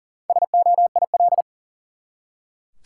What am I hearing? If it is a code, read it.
Morse code, heard as a keyed tone, about 40 wpm, 700 Hz. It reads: SOIL
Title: Four Letter Words - 40wpm